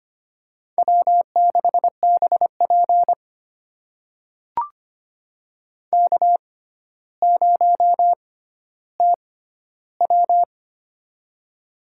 Morse code, heard as a keyed tone, about 25 wpm, spaced as for 13 wpm, 700 Hz. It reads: W6BP K0TW
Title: USA Call Signs - 4 Characters - 6x Character Spacing - 25wpm